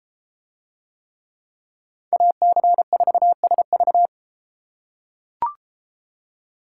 Morse code, heard as a keyed tone, about 30 wpm, 700 Hz. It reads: AC4SV